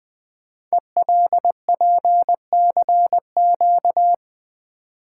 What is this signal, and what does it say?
Morse code, heard as a keyed tone, about 20 wpm, 700 Hz. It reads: ELPCQ
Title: Code Groups - 5 Letters - 20wpm